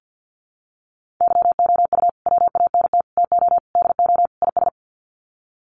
Morse code, heard as a keyed tone, about 50 wpm, 700 Hz. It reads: YOU WANT TO DO IS
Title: Sets of 5 Words Encore2 50wpm